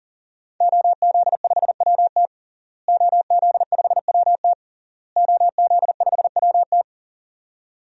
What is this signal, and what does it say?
Morse code, heard as a keyed tone, about 40 wpm, 700 Hz. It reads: OZ5WT OZ5WT OZ5WT
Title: DXCC-100 Call Signs - 5 Characters - Repeated 3x - 40wpm